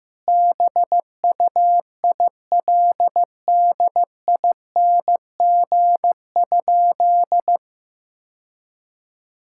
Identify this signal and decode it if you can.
Morse code, heard as a keyed tone, about 16 wpm, 700 Hz. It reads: BUILDING?
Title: Sentences from Top 1000 Words - Review Part 5 - 15wpm